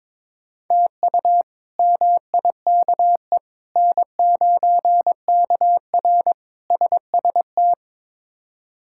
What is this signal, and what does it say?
Morse code, heard as a keyed tone, about 22 wpm, 700 Hz. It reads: TU MIKE N9KR SST
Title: K1USN SST QSO Practice - 22wpm